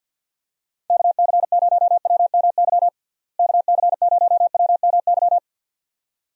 Morse code, heard as a keyed tone, about 51 wpm, 700 Hz. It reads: KC0WMY KC0WMY
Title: USA Call Signs - 6 Characters - Repeated 2x - 50wpm